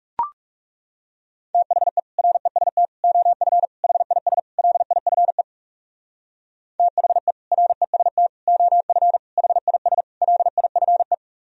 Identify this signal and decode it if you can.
Morse code, heard as a keyed tone, about 36 wpm, 700 Hz. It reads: THE REST OF HIS LIFE THE REST OF HIS LIFE
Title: Speed-Racing - Sets of 5 Words 30wpm